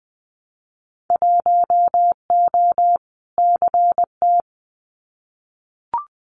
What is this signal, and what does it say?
Morse code, heard as a keyed tone, about 20 wpm, 700 Hz. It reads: 1O CT
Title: Winter Field Day Exchanges 20wpm